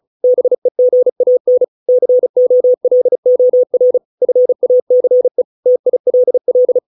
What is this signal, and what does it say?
Morse code, heard as a keyed tone, about 34 wpm, 500 Hz. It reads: BEGAN COLOR FACE TILL